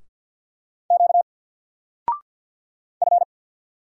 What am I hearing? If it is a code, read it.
Morse code, heard as a keyed tone, about 50 wpm, 700 Hz. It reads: Q F
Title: Single Character (Letter:Number:Punctuation:Prosign) 50wpm